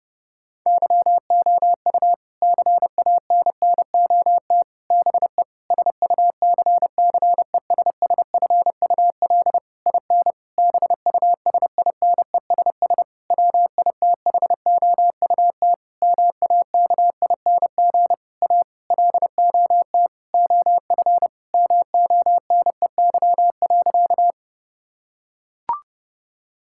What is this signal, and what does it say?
Morse code, heard as a keyed tone, about 30 wpm, 700 Hz. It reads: YOU CANNOT BE SUCCESSFUL IN BUSINESS WITHOUT MAKING A LOT OF MONEY.
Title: Sentences from Top 1000 Words - Review Part 6 - 30wpm